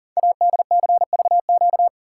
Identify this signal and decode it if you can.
Morse code, heard as a keyed tone, about 43 wpm, 700 Hz. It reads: ADCVQ
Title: Code Groups - 5 Letters - 40wpm